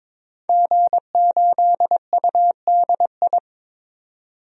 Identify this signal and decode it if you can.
Morse code, heard as a keyed tone, about 22 wpm, 700 Hz. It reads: G8UDI
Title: DXCC-100 Call Signs - 5 Characters - 22wpm